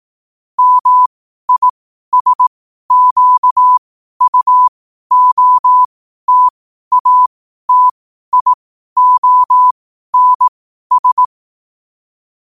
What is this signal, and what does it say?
Morse code, heard as a keyed tone, about 18 wpm, spaced as for 12 wpm, 1000 Hz. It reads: MISQUOTATIONS
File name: qotdfiles / 12wpm